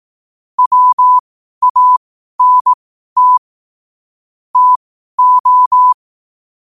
Morse code, heard as a keyed tone, about 18 wpm, spaced as for 12 wpm, 1000 Hz. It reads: WANT TO
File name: qotdfiles / 12wpm